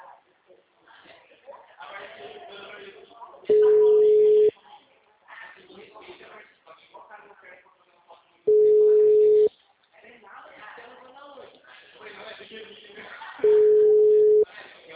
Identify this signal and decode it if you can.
Ringback tone (Europe, 425 Hz), with speech.